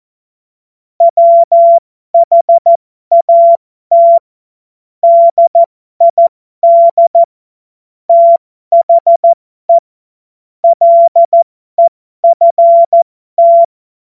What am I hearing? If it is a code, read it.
Morse code, heard as a keyed tone, about 14 wpm, spaced as for 12 wpm, 680 Hz. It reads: WHAT DID THE LEFT
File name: Morsekurs / Witze